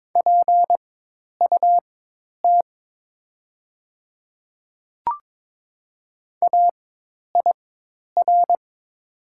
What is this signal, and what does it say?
Morse code, heard as a keyed tone, about 22 wpm, spaced as for 10 wpm, 700 Hz. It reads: PUT AIR
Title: 4x Character Spacing - Top 200 Words - 22wpm